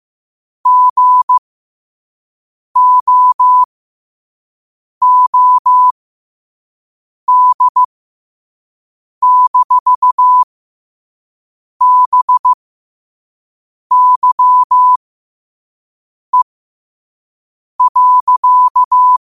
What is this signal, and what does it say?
Morse code, heard as a keyed tone, about 15 wpm, spaced as for 5 wpm, 1000 Hz. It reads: GOOD-BYE.
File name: qotdfiles / 5wpm